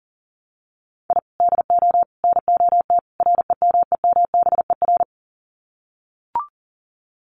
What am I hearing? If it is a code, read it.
Morse code, heard as a keyed tone, about 40 wpm, 700 Hz. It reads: I DO NOT REMEMBER
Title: Sets of 4 Words 40wpm